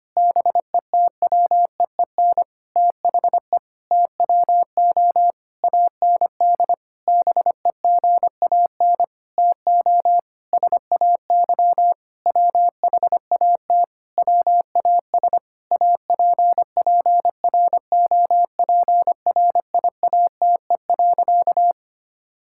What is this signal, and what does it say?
Morse code, heard as a keyed tone, about 25 wpm, 700 Hz. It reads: BETWEEN THE TWO AND BEGAN TO SAY WHAT WAS APPROPRIATE.
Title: Sentences from Top 1000 Words - Review Part 5 - 25wpm